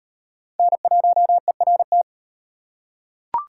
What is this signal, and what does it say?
Morse code, heard as a keyed tone, about 38 wpm, 700 Hz. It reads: N1ERT